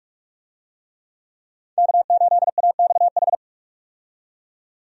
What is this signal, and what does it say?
Morse code, heard as a keyed tone, about 47 wpm, 700 Hz. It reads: K8AXH
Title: USA Call Signs - 5 Characters - 45wpm